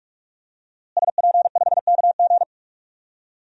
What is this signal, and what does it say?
Morse code, heard as a keyed tone, about 46 wpm, 700 Hz. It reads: SP5KG
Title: DXCC-100 Call Signs - 5 Characters - Repeated 2x - 45wpm